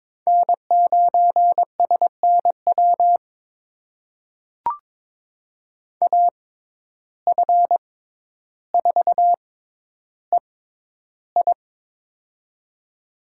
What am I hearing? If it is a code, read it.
Morse code, heard as a keyed tone, about 22 wpm, spaced as for 11 wpm, 700 Hz. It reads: N9SNW AF4EI
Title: USA Call Signs - 5 Characters - 6x Character Spacing - 22wpm